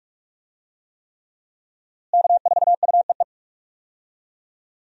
Morse code, heard as a keyed tone, about 44 wpm, 700 Hz. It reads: K4UEE
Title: USA Call Signs - 5 Characters - Repeated 3x - 45wpm